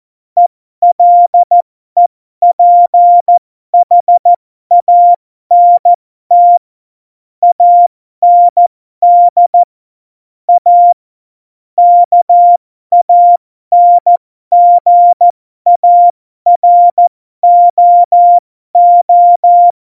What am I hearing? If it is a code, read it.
Morse code, heard as a keyed tone, about 14 wpm, spaced as for 12 wpm, 700 Hz. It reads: ELEPHANT AND A KANGAROO
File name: Morsekurs / Witze